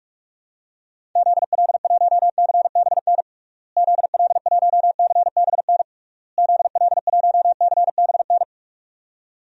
Morse code, heard as a keyed tone, about 46 wpm, 700 Hz. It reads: ZL1CBN ZL1CBN ZL1CBN
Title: DXCC-100 Call Signs - 6 Characters - Repeated 3x - 45wpm